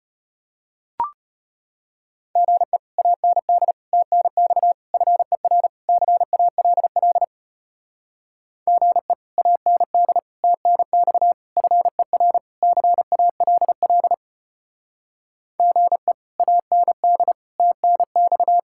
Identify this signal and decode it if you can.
Morse code, heard as a keyed tone, about 33 wpm, 700 Hz. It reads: GE AND TNX FER CALL GE AND TNX FER CALL GE AND TNX